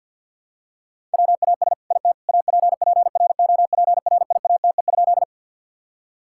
Morse code, heard as a keyed tone, about 50 wpm, 700 Hz. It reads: WAS IT APPROPRIATE?
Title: Sentences from Top 1000 Words - Review Part 5 - 50wpm